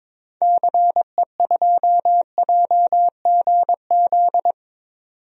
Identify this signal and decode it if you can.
Morse code, heard as a keyed tone, about 22 wpm, 700 Hz. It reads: CE2JGZ
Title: DXCC-100 Call Signs - 6 Characters - 22wpm